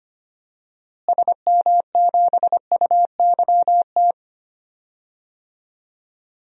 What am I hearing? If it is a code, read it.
Morse code, heard as a keyed tone, about 25 wpm, 700 Hz. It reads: SM7UYT
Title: DXCC-100 Call Signs - 6 Characters - Repeated 2x - 25wpm